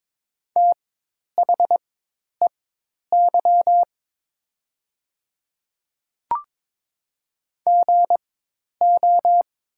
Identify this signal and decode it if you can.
Morse code, heard as a keyed tone, about 22 wpm, spaced as for 10 wpm, 700 Hz. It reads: THEY GO
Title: 4x Character Spacing - Top 300 Words - 22wpm